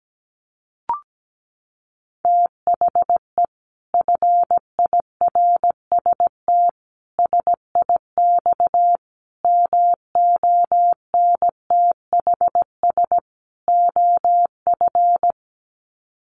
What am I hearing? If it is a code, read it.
Morse code, heard as a keyed tone, about 17 wpm, 700 Hz. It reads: THE FIRST SIX MONTHS OF
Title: Sets of 5 Words Encore2 17wpm